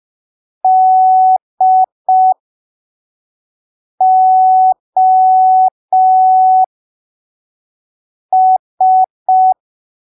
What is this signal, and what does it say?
Morse code, heard as a keyed tone, about 5 wpm, spaced as for 3 wpm, 750 Hz. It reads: DOS
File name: arquivos / texto / texto 08 05ppm